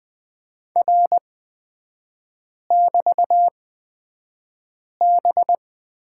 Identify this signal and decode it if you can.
Morse code, heard as a keyed tone, about 20 wpm, spaced as for 5 wpm, 700 Hz. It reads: R<BT>B
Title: 224 Lesson 36e - BT - Three Characters 20wpm